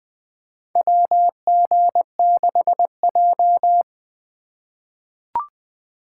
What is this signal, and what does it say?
Morse code, heard as a keyed tone, about 21 wpm, 700 Hz. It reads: WG6J
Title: USA Call Signs - 3x Character Spacing - 20wpm